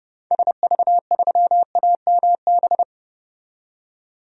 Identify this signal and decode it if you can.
Morse code, heard as a keyed tone, about 32 wpm, 700 Hz. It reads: SV3AMB
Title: DXCC-100 Call Signs - 6 Characters - 30wpm